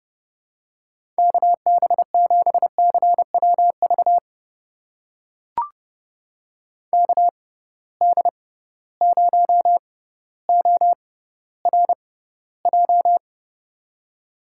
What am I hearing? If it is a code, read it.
Morse code, heard as a keyed tone, about 30 wpm, spaced as for 15 wpm, 700 Hz. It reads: KB7CWV KD0ORJ
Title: USA Call Signs - 6 Characters - 6x Character Spacing - 30wpm